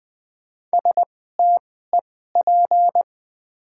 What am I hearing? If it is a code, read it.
Morse code, heard as a keyed tone, about 20 wpm, spaced as for 14 wpm, 700 Hz. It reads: STEP
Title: 2x Character Spacing - Top 500 Words - 20wpm